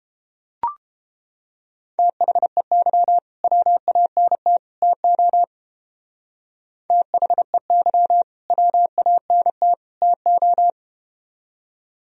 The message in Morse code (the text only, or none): THEY WANT TO THEY WANT TO